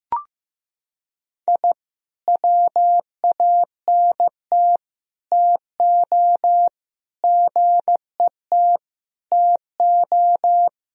Morse code, heard as a keyed tone, about 15 wpm, 700 Hz. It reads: I WANT TO GET TO
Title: Sentences from Top 1000 Words - Review Part 1 - 15wpm